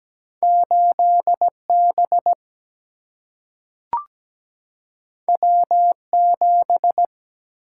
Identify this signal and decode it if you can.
Morse code, heard as a keyed tone, about 17 wpm, 700 Hz. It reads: 8 B W 7
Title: Single-Letter Single-Number 17wpm